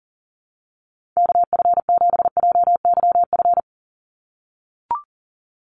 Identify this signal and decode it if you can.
Morse code, heard as a keyed tone, about 40 wpm, 700 Hz. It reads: KF7JYF
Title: USA Call Signs 40wpm